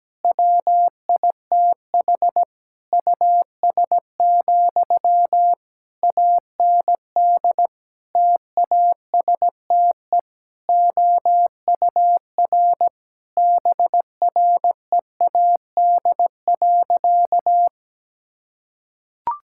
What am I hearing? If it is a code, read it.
Morse code, heard as a keyed tone, about 17 wpm, 700 Hz. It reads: WITH US, AND TASTE OUR BREAD.
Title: Sentences from Top 1600 Words 17wpm